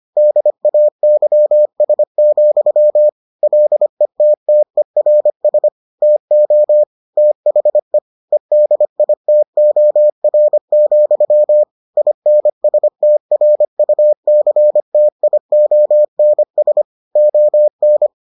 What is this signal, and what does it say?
Morse code, heard as a keyed tone, about 25 wpm, 600 Hz. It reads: DAYS, LETTERS TO THE EDITOR, INSTRUCTIONS ON